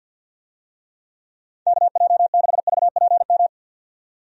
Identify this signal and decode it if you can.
Morse code, heard as a keyed tone, about 50 wpm, 700 Hz. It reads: KJ6VPM
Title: USA Call Signs - 6 Characters - 50wpm